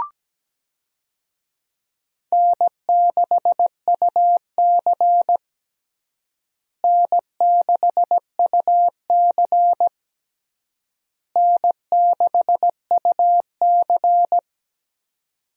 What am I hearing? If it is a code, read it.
Morse code, heard as a keyed tone, about 17 wpm, 700 Hz. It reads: N6UC N6UC N6UC
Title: USA Call Signs - 4 Characters - Repeated 3x - 17wpm